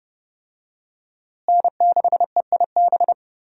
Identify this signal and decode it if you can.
Morse code, heard as a keyed tone, about 30 wpm, 700 Hz. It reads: N6EIB
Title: USA Call Signs - 5 Characters - Repeated 3x - 30wpm